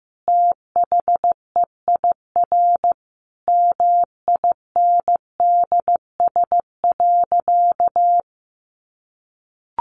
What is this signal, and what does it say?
Morse code, heard as a keyed tone, about 14 wpm, 700 Hz. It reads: THEIR MINDS.
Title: Easy Phrases 15wpm